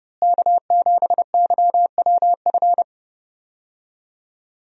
Morse code, heard as a keyed tone, about 30 wpm, 700 Hz. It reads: K7YWF
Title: USA Call Signs - 5 Characters - Repeated 2x - 30wpm